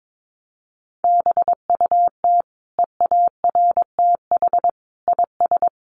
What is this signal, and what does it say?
Morse code, heard as a keyed tone, about 22 wpm, 700 Hz. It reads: BUT EARTH IS
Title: Sentences from Top 600 Words 22wpm